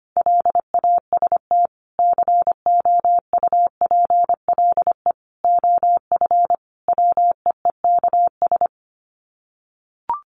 Morse code, heard as a keyed tone, about 25 wpm, 700 Hz. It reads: LAST COUPLE OF WEEKS
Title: Sets of 5 Words Encore2 25wpm